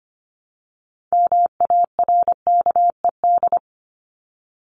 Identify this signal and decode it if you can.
Morse code, heard as a keyed tone, about 25 wpm, 700 Hz. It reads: MARKED
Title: ED Suffix - Top 100 Words with Base Words 25wpm